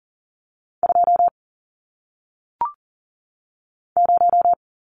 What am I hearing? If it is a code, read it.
Morse code, heard as a keyed tone, about 39 wpm, 700 Hz. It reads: 2 0
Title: Numbers - 1-Digit 40wpm